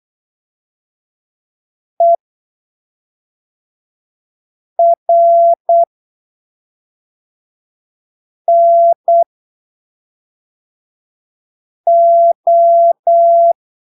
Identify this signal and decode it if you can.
Morse code, heard as a keyed tone, about 8 wpm, spaced as for 3 wpm, 680 Hz. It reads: ERNO
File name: arquivos / texto / texto 11 08ppm